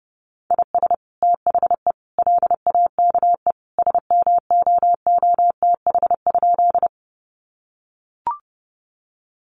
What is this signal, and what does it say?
Morse code, heard as a keyed tone, about 30 wpm, 700 Hz. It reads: IS THE LAKE SMOOTH?
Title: Sentences from Top 1400 Words 30wpm